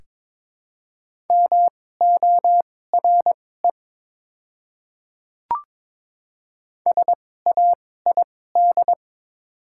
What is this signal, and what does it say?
Morse code, heard as a keyed tone, about 22 wpm, spaced as for 16 wpm, 690 Hz. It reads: MORE SAID
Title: 2x Character Spacing - Top 200 Words - 22wpm